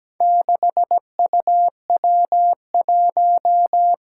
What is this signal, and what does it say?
Morse code, heard as a keyed tone, about 17 wpm, 700 Hz. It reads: 6UW1
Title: Four Challenging Letter-Number Combinations 17wpm